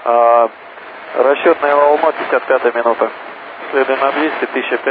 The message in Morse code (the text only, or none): none